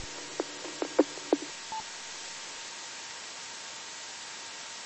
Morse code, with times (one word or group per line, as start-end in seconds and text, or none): none